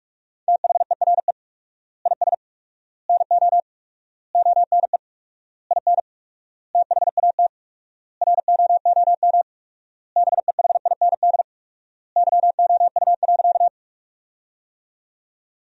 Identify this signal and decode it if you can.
Morse code, heard as a keyed tone, about 46 wpm, 700 Hz. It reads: THERE IS NO ONE IN THAT ROOM BEHIND YOU.
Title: 4x Word-Spacing - Sentences from Top 500 Words - 45wpm